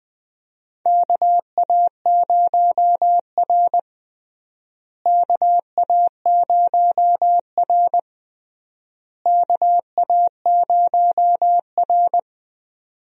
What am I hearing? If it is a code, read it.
Morse code, heard as a keyed tone, about 21 wpm, 700 Hz. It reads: KA0R KA0R KA0R
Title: USA Call Signs - 4 Characters - Repeated 3x - 20wpm